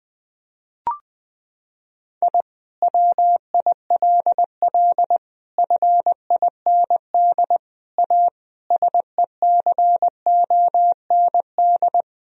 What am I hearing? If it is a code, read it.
Morse code, heard as a keyed tone, about 19 wpm, 700 Hz. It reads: I WILL FIND A SECOND
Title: Sentences from Top 1000 Words - Review Part 1 - 20wpm